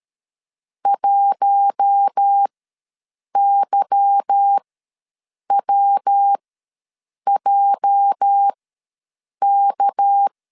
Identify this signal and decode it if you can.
Morse code, heard as a keyed tone, about 13 wpm, spaced as for 7 wpm, 800 Hz. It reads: 1YWJK
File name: LEZIONI / 8 YJ45 / 8 YJ ws 12wpm 3342